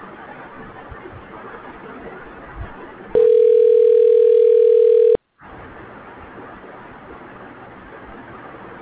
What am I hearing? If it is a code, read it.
Ringback tone (North America), 440 + 480 Hz.